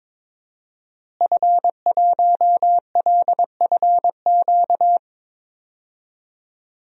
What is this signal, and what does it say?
Morse code, heard as a keyed tone, about 21 wpm, 700 Hz. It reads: F1LFQ
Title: DXCC-100 Call Signs - 5 Characters - Repeated 3x - 22wpm